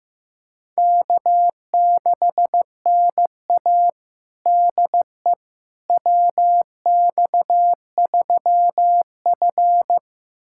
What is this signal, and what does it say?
Morse code, heard as a keyed tone, about 16 wpm, 700 Hz. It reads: K6NA DE WX3F
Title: USA Callsign DE Callsign - 4 Characters - 15wpm